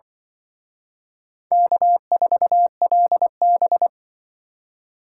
Morse code, heard as a keyed tone, about 24 wpm, 700 Hz. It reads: K4LB
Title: Speed Racing - USA 1x2 Call Signs 20wpm